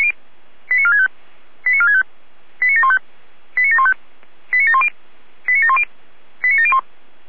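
Five-tone selective call (ZVEI), heard as quick runs of five tones, tones of about 70 ms. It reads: 89465 89465 89715 89715 89710 89710 89701
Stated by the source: ZVEI five-tone call 89531 received on SDR.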